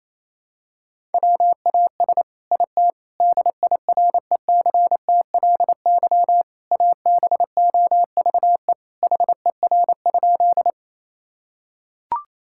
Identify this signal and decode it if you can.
Morse code, heard as a keyed tone, about 28 wpm, 700 Hz. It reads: WAS IT DIRECTLY ABOVE HER?
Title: Sentences from Top 1000 Words - Review Part 5 - 28wpm